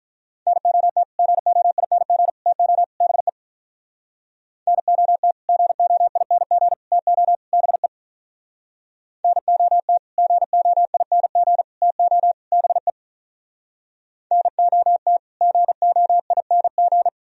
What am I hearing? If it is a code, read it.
Morse code, heard as a keyed tone, about 47 wpm, 700 Hz. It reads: NOT GOING TO BE NOT GOING TO BE NOT GOING TO BE NOT GOING